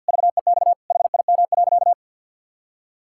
Morse code, heard as a keyed tone, about 51 wpm, 700 Hz. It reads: VEX HIM.
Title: Pangram Challenge v2 - 50wpm